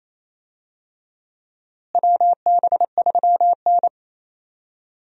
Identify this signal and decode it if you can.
Morse code, heard as a keyed tone, about 28 wpm, 700 Hz. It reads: WB3N
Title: USA Call Signs - 4 Characters - Repeated 2x - 28wpm